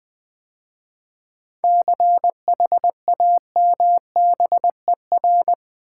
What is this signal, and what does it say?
Morse code, heard as a keyed tone, about 20 wpm, 700 Hz. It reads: CHAMBER
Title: 7 Letter Words - First 3 Letters Given - 20wpm